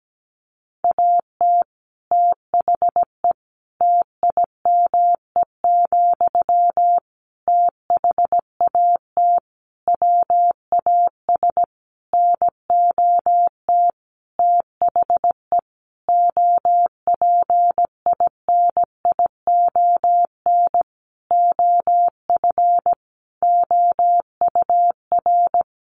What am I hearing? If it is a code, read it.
Morse code, heard as a keyed tone, about 17 wpm, 700 Hz. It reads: AT THE TIME, THAT WAS NOT THE OPINION OF OUR
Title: Sentences from Top 1400 Words 17wpm